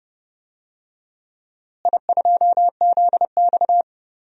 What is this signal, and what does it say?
Morse code, heard as a keyed tone, about 30 wpm, 700 Hz. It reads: I2ZX
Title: DXCC-100 Call Signs - 4 Characters - Repeated 2x - 30wpm